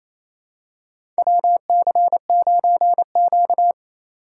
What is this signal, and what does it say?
Morse code, heard as a keyed tone, about 28 wpm, 700 Hz. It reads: WC9Q
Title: USA Call Signs - 4x Character Spacing - 28wpm